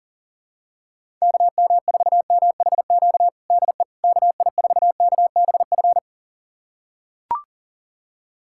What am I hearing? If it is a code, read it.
Morse code, heard as a keyed tone, about 37 wpm, 700 Hz. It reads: KM4MHQ DE KI4KBF
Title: USA Callsign DE Callsign - 6 Characters - 40wpm